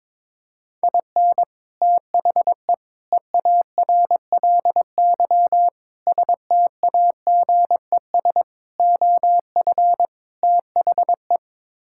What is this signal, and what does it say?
Morse code, heard as a keyed tone, about 22 wpm, 700 Hz. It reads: IN THE EARLY STAGES OF THE
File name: practice / 22wpm / Sets of 6 Words - 22wpm